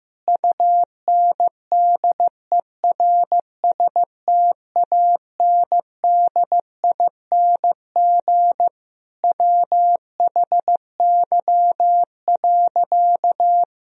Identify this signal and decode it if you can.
Morse code, heard as a keyed tone, about 15 wpm, 700 Hz. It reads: UNDERSTANDING WHY.
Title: Sentences from Top 1000 Words - Review Part 3 - 15wpm